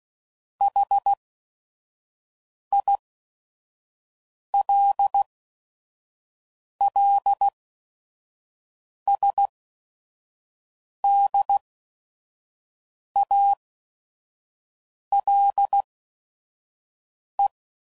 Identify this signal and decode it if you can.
Morse code, heard as a keyed tone, about 16 wpm, spaced as for 5 wpm, 800 Hz. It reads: HILLSDALE
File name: CW Code Groups @5WPM / fromSM's / Cities